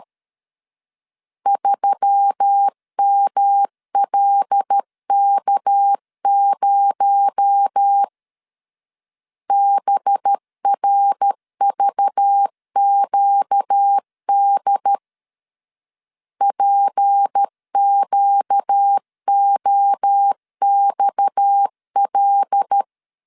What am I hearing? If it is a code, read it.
Morse code, heard as a keyed tone, about 13 wpm, 800 Hz. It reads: 3MLK0 BRVQD PQOXL
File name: CW / DET / 1 6 1500C S A 12wpm 3440